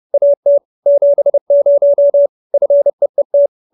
Morse code, heard as a keyed tone, about 30 wpm, 570 Hz. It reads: AT 70 FEET
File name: QSO311 30